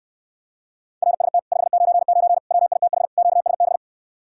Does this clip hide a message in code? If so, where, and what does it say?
Yes: Morse code from 1.0 to 3.8 s, heard as a keyed tone, about 69 wpm, 700 Hz. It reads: RST 599 WITH QSB